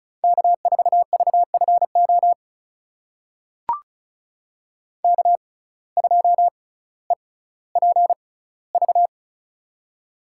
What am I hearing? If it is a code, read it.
Morse code, heard as a keyed tone, about 36 wpm, spaced as for 18 wpm, 700 Hz. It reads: K4VFO K2EPV
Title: USA Call Signs - 5 Characters - 6x Character Spacing - 35wpm